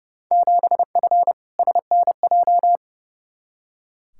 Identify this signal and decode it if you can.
Morse code, heard as a keyed tone, about 30 wpm, 700 Hz. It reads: 7F SNJ
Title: Field Day Exchanges Encore - 30wpm